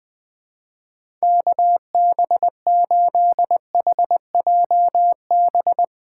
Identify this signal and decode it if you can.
Morse code, heard as a keyed tone, about 20 wpm, 700 Hz. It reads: KB8HJB
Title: USA Call Signs - 6 Characters - 4x Character Spacing - 20wpm